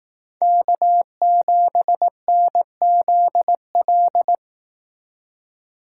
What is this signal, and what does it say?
Morse code, heard as a keyed tone, about 19 wpm, 700 Hz. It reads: K7NZL